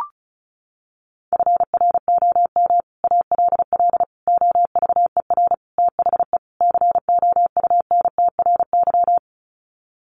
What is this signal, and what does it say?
Morse code, heard as a keyed tone, about 34 wpm, 700 Hz. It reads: FROM ALL OVER THE COUNTRY
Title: Sets of 5 Words Encore 35wpm